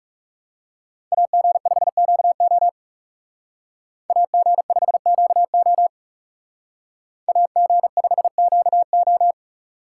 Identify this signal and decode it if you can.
Morse code, heard as a keyed tone, about 38 wpm, 700 Hz. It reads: AG5QO AG5QO AG5QO